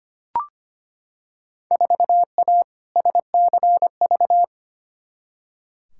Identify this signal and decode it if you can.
Morse code, heard as a keyed tone, about 25 wpm, 700 Hz. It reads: 4A SCV
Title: Field Day Exchanges Encore - 25wpm